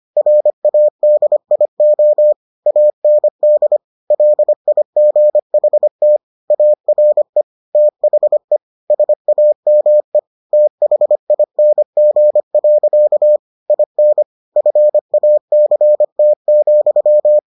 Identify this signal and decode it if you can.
Morse code, heard as a keyed tone, about 25 wpm, 600 Hz. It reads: RADIO AND LIGHT ARE THE SAME THING. IN FACT,